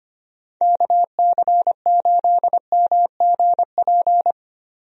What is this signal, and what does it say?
Morse code, heard as a keyed tone, about 25 wpm, 700 Hz. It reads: KC8MGP
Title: USA Call Signs - 6 Characters - Repeated 2x - 25wpm